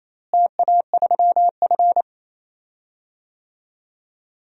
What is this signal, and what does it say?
Morse code, heard as a keyed tone, about 28 wpm, 700 Hz. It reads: TA3F
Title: DXCC-100 Call Signs - 4 Characters - Repeated 2x - 28wpm